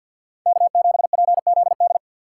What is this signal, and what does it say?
Morse code, heard as a keyed tone, about 64 wpm, 700 Hz. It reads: K7PZD